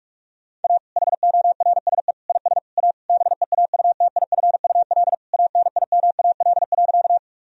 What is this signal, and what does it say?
Morse code, heard as a keyed tone, about 46 wpm, 700 Hz. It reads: A HORSE IS A BEAUTIFUL ANIMAL.
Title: Sentences from Top 1000 Words - Review Part 1 - 45wpm